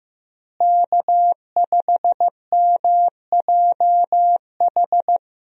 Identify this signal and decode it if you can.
Morse code, heard as a keyed tone, about 15 wpm, 700 Hz. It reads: K5MJH
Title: USA Call Signs - 5 Characters - Repeated 2x - 15wpm